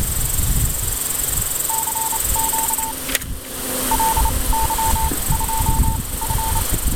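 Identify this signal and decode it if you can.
Morse code, heard as a keyed tone, about 30 wpm, spaced as for 22 wpm, 890 Hz. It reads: CQ LY2F
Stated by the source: Morse QSO on 7030 kHz between LY2F and UA1CEG using a homemade superheterodyne receiver with converter, operated outdoors; strong wind noise audible at times.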